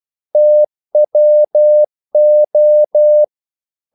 Morse code, heard as a keyed tone, about 12 wpm, 600 Hz. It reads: TWO
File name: CW / Practice / If12